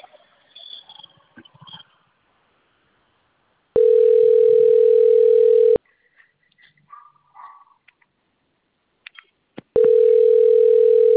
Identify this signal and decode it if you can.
Ringback tone (North America), 440 + 480 Hz.